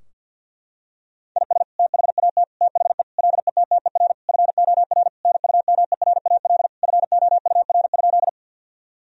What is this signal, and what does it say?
Morse code, heard as a keyed tone, about 50 wpm, 700 Hz. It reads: IS THAT THE LETTER FOR NUMERAL FOUR?
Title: Sentences from Top 1000 Words - Review Part 2 - 50wpm